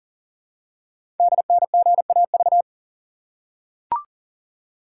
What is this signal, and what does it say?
Morse code, heard as a keyed tone, about 42 wpm, 690 Hz. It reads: DNGAV
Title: Code Groups - 5 Letters - 40wpm